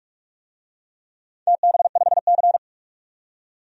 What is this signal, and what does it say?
Morse code, heard as a keyed tone, about 44 wpm, 700 Hz. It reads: TB5C
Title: DXCC-100 Call Signs - 4 Characters - 45wpm